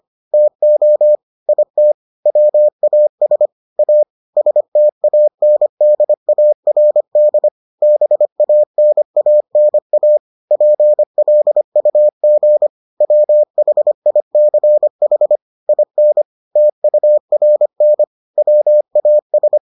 Morse code, heard as a keyed tone, about 25 wpm, 600 Hz. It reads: TO IT WAS A STANDARD BANANA PLUG WHICH IN TURN WAS